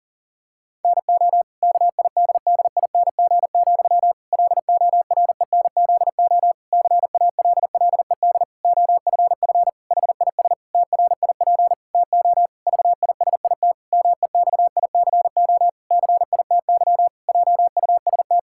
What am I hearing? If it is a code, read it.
Morse code, heard as a keyed tone, about 40 wpm, 700 Hz. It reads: NO KIDDING, LORENZO CALLED OFF HIS TRIP TO VISIT MEXICO CITY JUST
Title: Pangram Challenge v2 - 40wpm